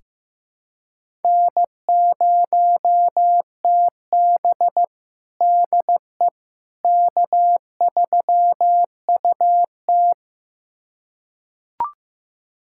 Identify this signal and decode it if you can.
Morse code, heard as a keyed tone, about 15 wpm, 700 Hz. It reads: N0TB DE K3UT
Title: USA Callsign DE Callsign - 4 Characters - 15wpm